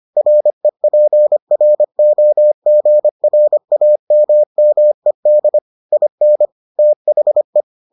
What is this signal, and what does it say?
Morse code, heard as a keyed tone, about 25 wpm, 600 Hz. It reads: REPROGRAMMED IN THE